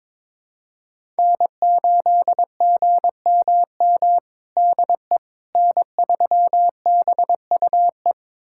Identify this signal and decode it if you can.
Morse code, heard as a keyed tone, about 21 wpm, 700 Hz. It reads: N8GMM DE N3BUE
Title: USA Callsign DE Callsign - 5 Characters - 22wpm